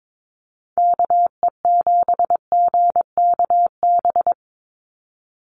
Morse code, heard as a keyed tone, about 22 wpm, 700 Hz. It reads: KE7GKB
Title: USA Call Signs 22wpm